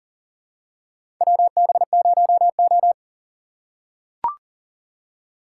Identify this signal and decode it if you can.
Morse code, heard as a keyed tone, about 40 wpm, 700 Hz. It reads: WB0O
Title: USA Call Signs - 4 Characters - 4x Character Spacing - 40wpm